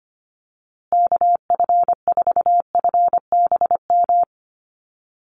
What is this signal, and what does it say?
Morse code, heard as a keyed tone, about 25 wpm, 700 Hz. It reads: KF4FBM
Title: USA Call Signs Repeated 2x 25wpm